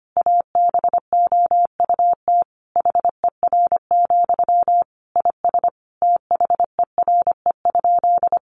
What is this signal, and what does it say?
Morse code, heard as a keyed tone, about 25 wpm, 700 Hz. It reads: ABOUT HER, IS THERE?
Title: Sentences from Top 1500 Words 25wpm